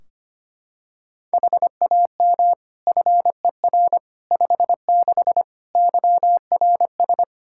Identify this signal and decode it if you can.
Morse code, heard as a keyed tone, about 25 wpm, 700 Hz. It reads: HAM FER 56 YRS
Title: Practice QSO - Exchange 3 - More Info - 25wpm